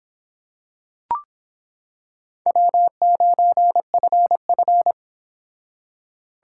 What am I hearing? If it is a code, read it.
Morse code, heard as a keyed tone, about 26 wpm, 700 Hz. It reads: W9FF